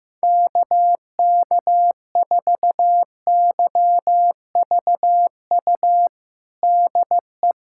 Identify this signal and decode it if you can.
Morse code, heard as a keyed tone, about 16 wpm, 700 Hz. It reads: KK4YVU DE
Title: USA Callsign DE Callsign 15wpm